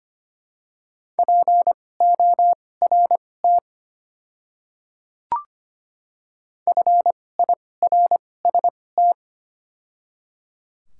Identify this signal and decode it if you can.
Morse code, heard as a keyed tone, about 25 wpm, spaced as for 18 wpm, 700 Hz. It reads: PORT FIRST
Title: 2x Character Spacing - Top 200 Words - 25wpm